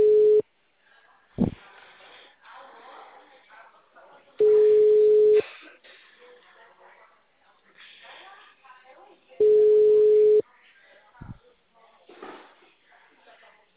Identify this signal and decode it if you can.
Ringback tone (Europe, 425 Hz), with speech.